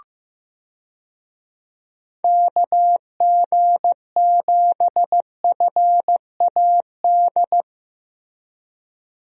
Morse code, heard as a keyed tone, about 15 wpm, 700 Hz. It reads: KG7FAD
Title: USA Call Signs - 6 Characters - Repeated 2x - 15wpm